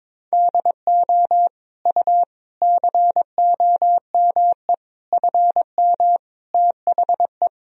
Morse code, heard as a keyed tone, about 22 wpm, 700 Hz. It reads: DO U COME FM THE
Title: Sentences from Top 200 Words and Abbreviations 22wpm